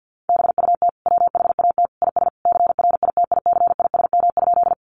Morse code, heard as a keyed tone, about 48 wpm, 700 Hz. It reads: BUT WHAT IS CRITICISM?
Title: Sentences from Top 1300 Words 50wpm